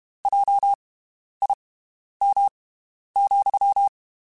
Morse code, heard as a keyed tone, about 32 wpm, spaced as for 11 wpm, 800 Hz. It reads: JIM,
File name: (spaced) / QSOs Spaced-mp3s / WD0BGP 1 @30 sp